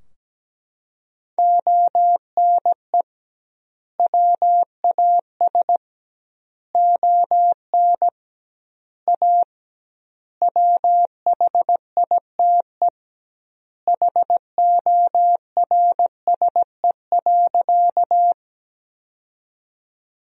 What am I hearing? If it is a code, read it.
Morse code, heard as a keyed tone, about 17 wpm, 700 Hz. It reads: ONE WAS ON A WHITE HORSE.
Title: 2x Word-Spacing - Sentences from Top 300 Words - 17wpm